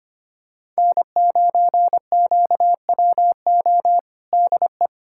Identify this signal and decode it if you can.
Morse code, heard as a keyed tone, about 25 wpm, 700 Hz. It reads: N9QWO DE
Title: USA Callsign DE Callsign - 5 Characters - 25wpm